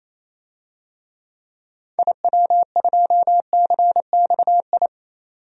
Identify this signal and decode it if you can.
Morse code, heard as a keyed tone, about 27 wpm, 700 Hz. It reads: IW2CXI
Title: DXCC-100 Call Signs - 6 Characters - 28wpm